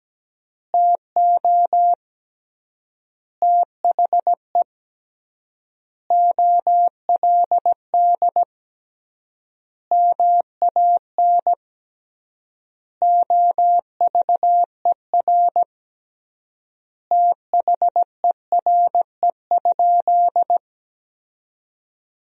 Morse code, heard as a keyed tone, about 17 wpm, 700 Hz. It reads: TO THE OLD MAN OVER THERE?
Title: 3x Word-Spacing - Sentences from Top 200 Words - 17wpm